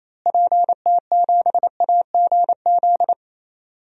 Morse code, heard as a keyed tone, about 28 wpm, 700 Hz. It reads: PT7AGZ
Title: DXCC-100 Call Signs - 6 Characters - Repeated 3x - 28wpm